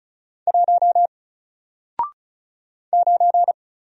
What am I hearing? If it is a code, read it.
Morse code, heard as a keyed tone, about 35 wpm, 700 Hz. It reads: 1 9
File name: Single-Letter-Number-35wpm